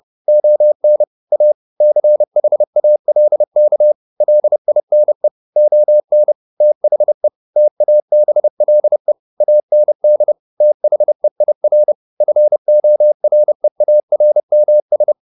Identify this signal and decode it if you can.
Morse code, heard as a keyed tone, about 30 wpm, 600 Hz. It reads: ON A CHALK LINE ON THE TABLE AND THEIR FOREARMS
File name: the old man and the sea / the old man and the sea 1 30wpm